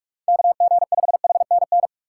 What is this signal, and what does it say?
Morse code, heard as a keyed tone, about 46 wpm, 700 Hz. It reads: KG5HNN
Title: USA Call Signs - 3x Character Spacing - 45wpm